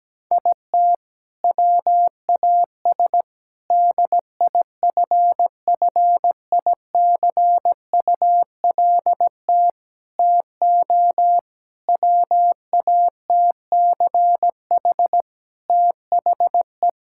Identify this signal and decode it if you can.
Morse code, heard as a keyed tone, about 17 wpm, 700 Hz. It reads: IT WAS DIFFICULT TO WATCH THE
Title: Sentences from Top 1000 Words - Review Part 5 - 17wpm